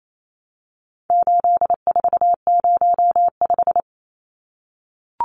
Morse code, heard as a keyed tone, about 28 wpm, 700 Hz. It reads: 8405
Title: Numbers - 4-Digit 28wpm